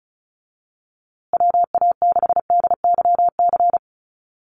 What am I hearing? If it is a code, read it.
Morse code, heard as a keyed tone, about 34 wpm, 700 Hz. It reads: WA6DYC
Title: USA Call Signs Repeated 3x 35wpm